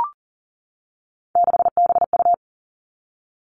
Morse code, heard as a keyed tone, about 40 wpm, 700 Hz. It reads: 6BU
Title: All Cut Numbers - 3-Digit 40wpm